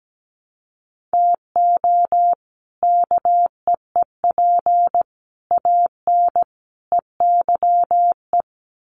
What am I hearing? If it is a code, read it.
Morse code, heard as a keyed tone, about 17 wpm, 700 Hz. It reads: TO KEEP AN EYE
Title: Sets of 5 Words Encore 17wpm